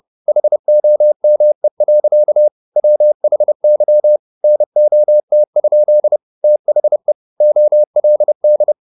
Morse code, heard as a keyed tone, about 30 wpm, 600 Hz. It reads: HOME. WHY NOT? THE OLD
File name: the old man and the sea / the old man and the sea 1 30wpm